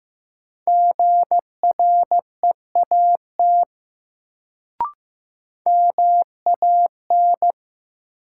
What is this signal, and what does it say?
Morse code, heard as a keyed tone, about 15 wpm, 700 Hz. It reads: GREAT MAN
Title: Sets of 2 Words - New Word Drill - 15wpm